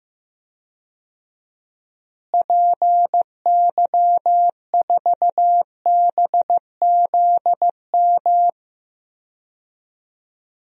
Morse code, heard as a keyed tone, about 16 wpm, 700 Hz. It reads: PY4BZM
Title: DXCC-100 Call Signs - 6 Characters - Repeated 3x - 15wpm